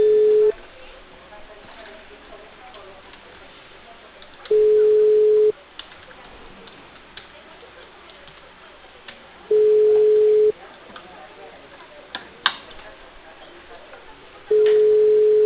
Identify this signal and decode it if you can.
Ringback tone (Europe, 425 Hz), with speech.